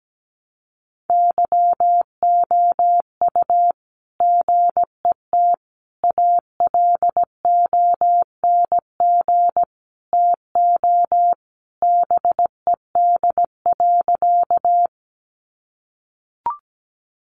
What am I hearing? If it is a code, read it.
Morse code, heard as a keyed tone, about 17 wpm, 700 Hz. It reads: YOU GET ALONG TO BED.
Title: Sentences from Top 700 Words 17wpm